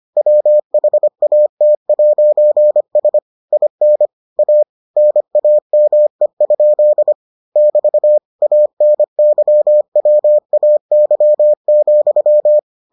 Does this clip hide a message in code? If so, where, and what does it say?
Yes: Morse code from 0.2 to 12.6 s, heard as a keyed tone, about 25 wpm, 600 Hz. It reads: WHAT'S IN A NAME? <BT> ANYWAY,